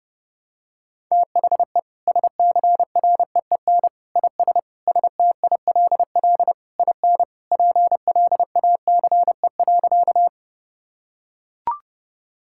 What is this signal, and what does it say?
Morse code, heard as a keyed tone, about 32 wpm, 700 Hz. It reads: THE SCREEN IS STILL IN PLACE.
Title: Sentences from Top 1000 Words - Review Part 6 - 30wpm